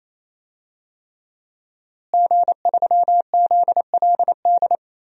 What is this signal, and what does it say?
Morse code, heard as a keyed tone, about 28 wpm, 700 Hz. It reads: G3ZLD
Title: DXCC-100 Call Signs - 5 Characters - Repeated 3x - 28wpm